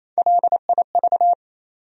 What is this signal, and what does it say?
Morse code, heard as a keyed tone, about 28 wpm, 700 Hz. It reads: LIV
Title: USA Common Names - Max 3 Letters - 28wpm